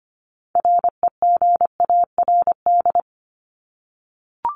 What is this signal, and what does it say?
Morse code, heard as a keyed tone, about 25 wpm, 700 Hz. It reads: REGARD
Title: ING Suffix - Top 100 Words with Base Words 25wpm